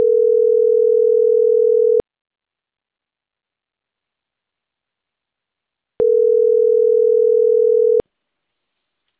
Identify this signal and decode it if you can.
Ringback tone (North America), 440 + 480 Hz.